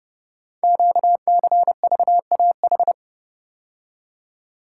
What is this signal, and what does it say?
Morse code, heard as a keyed tone, about 29 wpm, 700 Hz. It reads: QCVAH
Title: Code Groups - 5 Letters - 30wpm